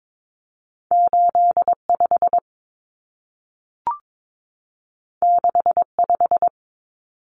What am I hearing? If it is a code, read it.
Morse code, heard as a keyed tone, about 22 wpm, 700 Hz. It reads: 85 65
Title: Common Cut Numbers - 2-Digit 22wpm